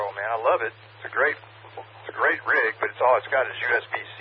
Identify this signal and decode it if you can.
Emergency Alert System (SAME) tones, heard as harsh buzzing bursts, with speech. It is heard from 0.9 to 3.9 s.